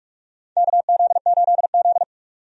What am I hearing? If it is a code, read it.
Morse code, heard as a keyed tone, about 42 wpm, 700 Hz. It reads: KZ8Z
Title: USA Call Signs - 4 Characters - 45wpm